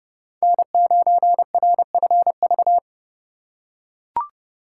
Morse code, heard as a keyed tone, about 28 wpm, 700 Hz. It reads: N9RFV